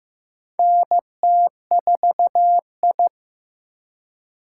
Morse code, heard as a keyed tone, about 15 wpm, 700 Hz. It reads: NT4I
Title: USA Call Signs - 4 Characters - 2x Character Spacing - 15wpm